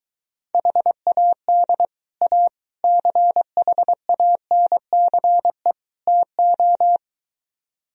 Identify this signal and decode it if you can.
Morse code, heard as a keyed tone, about 23 wpm, 700 Hz. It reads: HAD A CHANCE TO